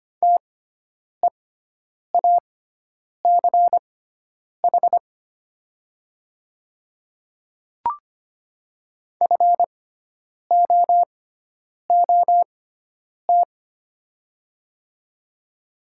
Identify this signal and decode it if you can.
Morse code, heard as a keyed tone, about 25 wpm, spaced as for 9 wpm, 700 Hz. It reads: TEACH FOOT
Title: 6x Character Spacing - Top 500 Words - 25wpm